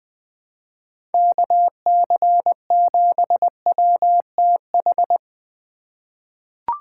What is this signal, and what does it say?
Morse code, heard as a keyed tone, about 20 wpm, 700 Hz. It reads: KC7WTH
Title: USA Call Signs - 3x Character Spacing - 20wpm